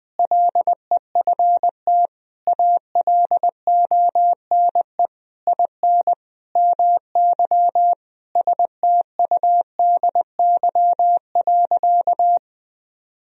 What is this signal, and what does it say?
Morse code, heard as a keyed tone, about 20 wpm, 700 Hz. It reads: LEFT ALONE IN MY STUDY.
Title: Sentences from Top 1000 Words - Review Part 1 - 20wpm